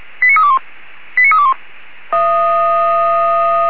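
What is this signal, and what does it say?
Five-tone selective call (ZVEI), heard as quick runs of five tones, tones of about 70 ms. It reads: 89321 89321
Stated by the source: ZVEI five-tone selective call received by SDR.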